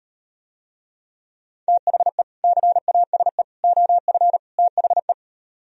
Morse code, heard as a keyed tone, about 38 wpm, 700 Hz. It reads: THE CASE OF THE